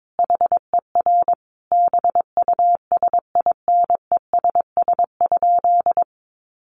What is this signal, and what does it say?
Morse code, heard as a keyed tone, about 22 wpm, 700 Hz. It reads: HER BUSINESS?
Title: Sentences from Top 1100 Words 22wpm